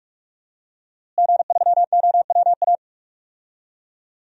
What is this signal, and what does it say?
Morse code, heard as a keyed tone, about 46 wpm, 700 Hz. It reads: G3OWA
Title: DXCC-100 Call Signs - 5 Characters - Repeated 3x - 45wpm